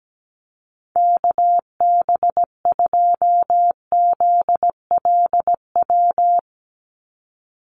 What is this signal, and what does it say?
Morse code, heard as a keyed tone, about 17 wpm, 700 Hz. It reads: KB2ZLW
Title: USA Call Signs 17wpm